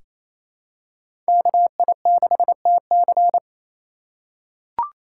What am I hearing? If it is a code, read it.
Morse code, heard as a keyed tone, about 28 wpm, 700 Hz. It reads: KI6TC
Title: USA Call Signs - 4x Character Spacing - 28wpm